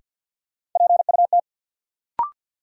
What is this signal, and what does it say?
Morse code, heard as a keyed tone, about 51 wpm, 700 Hz. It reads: PUT
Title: Sentences from Top 200 Words - New Word Drill - 50wpm